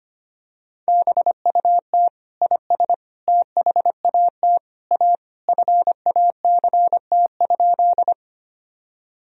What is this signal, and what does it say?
Morse code, heard as a keyed tone, about 25 wpm, 700 Hz. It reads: BUT IS THAT A FACT?
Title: Sentences from Top 1000 Words - Review Part 3 - 25wpm